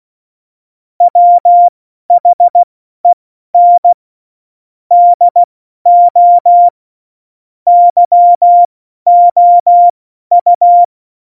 Morse code, heard as a keyed tone, about 16 wpm, spaced as for 12 wpm, 700 Hz. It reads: WHEN DO YOU
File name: Morsekurs / Witze